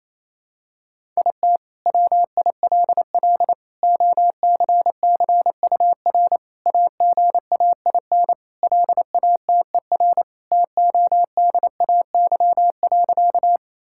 Morse code, heard as a keyed tone, about 28 wpm, 700 Hz. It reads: IT WILL OCCUR AGAIN LATER TODAY.
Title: Sentences from Top 1000 Words - Review Part 6 - 28wpm